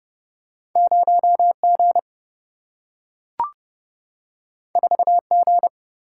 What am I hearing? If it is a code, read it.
Morse code, heard as a keyed tone, about 29 wpm, 700 Hz. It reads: 0G 4G
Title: Single-Letter Single-Number 30wpm